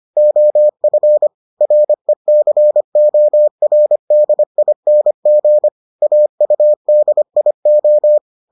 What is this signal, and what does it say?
Morse code, heard as a keyed tone, about 25 wpm, 600 Hz. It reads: OF RECORDING AUDIO